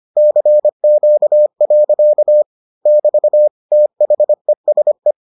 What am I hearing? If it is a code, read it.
Morse code, heard as a keyed tone, about 25 wpm, 600 Hz. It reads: CQ. <BT> THESE